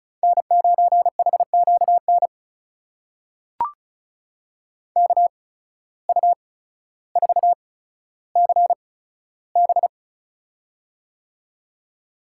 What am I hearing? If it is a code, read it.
Morse code, heard as a keyed tone, about 34 wpm, spaced as for 15 wpm, 700 Hz. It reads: N9HQN KU4CB
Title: USA Call Signs - 5 Characters - 8x Character Spacing - 35wpm